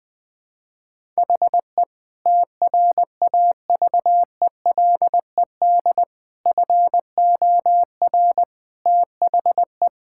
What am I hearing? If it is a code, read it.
Morse code, heard as a keyed tone, about 20 wpm, 700 Hz. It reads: HE TRAVELED FOR THE
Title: Sentences with 6-Word Phrases - 20wpm